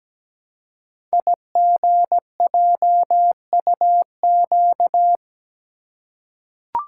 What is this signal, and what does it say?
Morse code, heard as a keyed tone, about 17 wpm, 700 Hz. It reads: IGJUQ
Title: Code Groups - 5 Letters - 17wpm